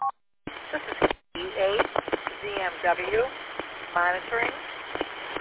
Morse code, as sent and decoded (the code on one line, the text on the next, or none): none